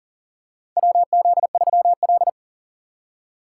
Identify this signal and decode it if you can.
Morse code, heard as a keyed tone, about 41 wpm, 700 Hz. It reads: WZ3L
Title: USA Call Signs - 4 Characters - 3x Character Spacing - 40wpm